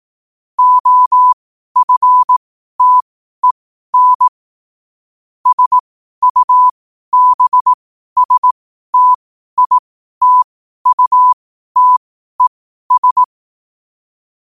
Morse code, heard as a keyed tone, about 18 wpm, spaced as for 12 wpm, 1000 Hz. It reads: OFTEN SUBSTITUTES
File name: qotdfiles / 12wpm